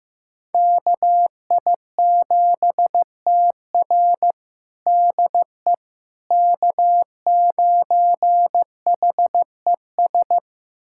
Morse code, heard as a keyed tone, about 15 wpm, 700 Hz. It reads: KI7TR DE K9HES
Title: USA Callsign DE Callsign - 5 Characters - 15wpm